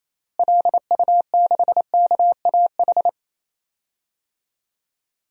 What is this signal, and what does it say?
Morse code, heard as a keyed tone, about 28 wpm, 700 Hz. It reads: LU6KAH
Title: DXCC-100 Call Signs - 6 Characters - Repeated 2x - 28wpm